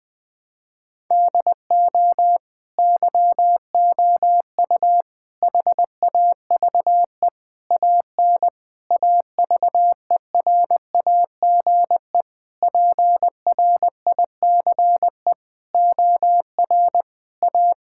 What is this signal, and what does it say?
Morse code, heard as a keyed tone, about 20 wpm, 700 Hz. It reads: DO YOU HAVE AN AVERAGE PRICE OR A
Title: Sets of 9 Words - 20wpm